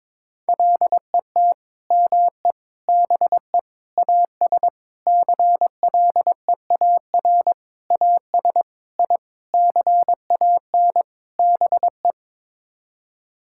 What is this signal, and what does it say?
Morse code, heard as a keyed tone, about 22 wpm, 700 Hz. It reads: LET ME BE AS CLEAR AS I CAN BE
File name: Sets of 9 Words - 22wpm